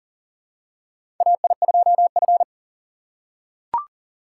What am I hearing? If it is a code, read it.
Morse code, heard as a keyed tone, about 40 wpm, 700 Hz. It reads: AI2F
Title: USA Call Signs - 4 Characters - 4x Character Spacing - 40wpm